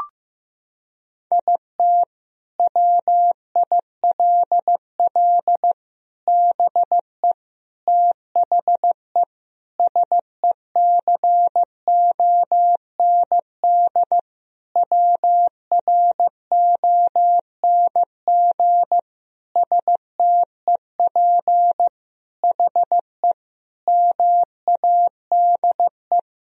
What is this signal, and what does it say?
Morse code, heard as a keyed tone, about 15 wpm, 700 Hz. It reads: IT WILL BE THE SECOND WRONG STEP HE MADE
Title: Sentences from Top 1000 Words - Review Part 2 - 15wpm